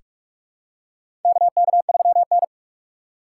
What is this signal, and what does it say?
Morse code, heard as a keyed tone, about 47 wpm, 700 Hz. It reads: KK3N
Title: USA Call Signs - 4 Characters - 2x Character Spacing - 45wpm